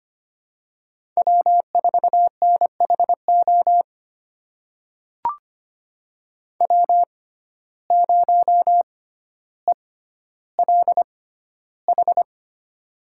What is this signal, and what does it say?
Morse code, heard as a keyed tone, about 25 wpm, spaced as for 13 wpm, 700 Hz. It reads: W4NHO W0ELH
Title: USA Call Signs - 5 Characters - 6x Character Spacing - 25wpm